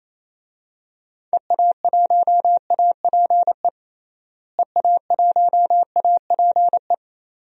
Morse code, heard as a keyed tone, about 28 wpm, 700 Hz. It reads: EA1APE EA1APE
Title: DXCC-100 Call Signs - 6 Characters - Repeated 2x - 28wpm